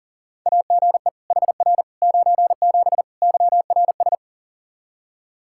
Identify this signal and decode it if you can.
Morse code, heard as a keyed tone, about 40 wpm, 700 Hz. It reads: AGE HR 97 YRS
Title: Practice QSO - Exchange 3 - More Info - 40wpm